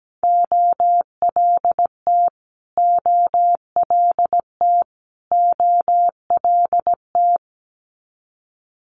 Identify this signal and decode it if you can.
Morse code, heard as a keyed tone, about 17 wpm, 700 Hz. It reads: OLT OLT OLT
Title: Multi-letter Phonemes Repeated 3x 17wpm